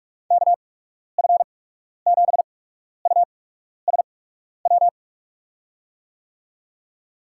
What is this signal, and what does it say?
Morse code, heard as a keyed tone, about 48 wpm, spaced as for 12 wpm, 700 Hz. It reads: KF7USW
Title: USA Call Signs - 6 Characters - 8x Character Spacing - 45wpm